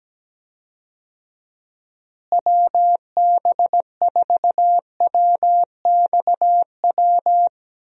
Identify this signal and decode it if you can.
Morse code, heard as a keyed tone, about 17 wpm, 700 Hz. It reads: WB4WXW
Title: USA Call Signs - 6 Characters - Repeated 2x - 17wpm